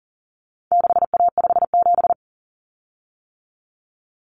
Morse code, heard as a keyed tone, about 40 wpm, 700 Hz. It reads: 6A57
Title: Common Cut Numbers - 4-Digit 40wpm